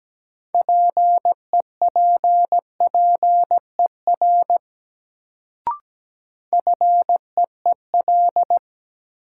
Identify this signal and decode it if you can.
Morse code, heard as a keyed tone, about 17 wpm, 700 Hz. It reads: PEPPER FEEL
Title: Sets of 5 Words - New Word Drill - 17wpm